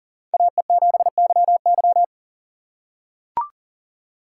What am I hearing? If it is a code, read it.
Morse code, heard as a keyed tone, about 40 wpm, 700 Hz. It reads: AE7YY
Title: USA Call Signs - 5 Characters - 4x Character Spacing - 40wpm